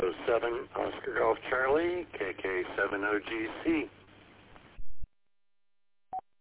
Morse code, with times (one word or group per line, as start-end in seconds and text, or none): none